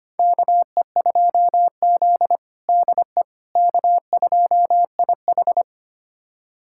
Morse code, heard as a keyed tone, about 25 wpm, 700 Hz. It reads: KE2Z DE K2IH
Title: USA Callsign DE Callsign - 4 Characters - 25wpm